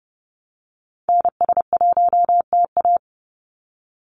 Morse code, heard as a keyed tone, about 30 wpm, 700 Hz. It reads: NS1TA
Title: NA SOTA Activator Callsigns 30wpm